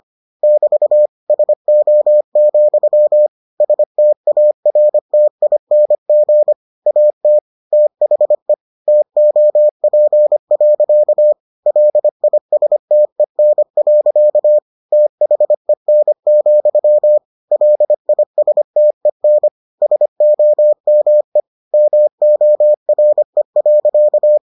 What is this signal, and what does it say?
Morse code, heard as a keyed tone, about 25 wpm, 600 Hz. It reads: <BT> SO, STARTING AT THE TOP. LISTEN. THEN, LISTEN SOME MORE.